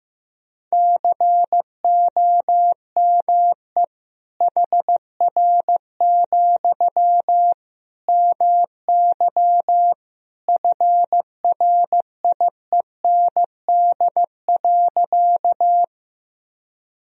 Morse code, heard as a keyed tone, about 15 wpm, 700 Hz. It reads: COME HR, MY FRIEND.
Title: Sentences from Top 300 Words and Abbreviations 15wpm